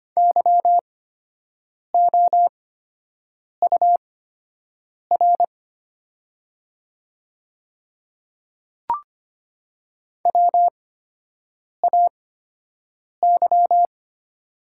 Morse code, heard as a keyed tone, about 25 wpm, spaced as for 7 wpm, 700 Hz. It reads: YOUR WAY
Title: 8x Character Spacing - Top 100 Words - 25wpm